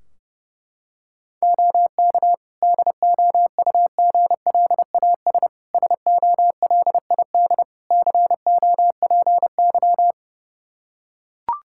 Morse code, heard as a keyed tone, about 30 wpm, 700 Hz. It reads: OK DOUGLAS SOLID COPY
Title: Practice QSO - Exchange 3 - More Info - 30wpm